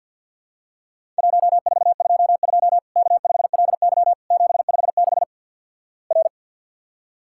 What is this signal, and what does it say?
Morse code, heard as a keyed tone, about 50 wpm, 700 Hz. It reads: 1422 K5LY 756 R
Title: Worked All Europe DX Contest - QTC Exchange with Repeats - 50wpm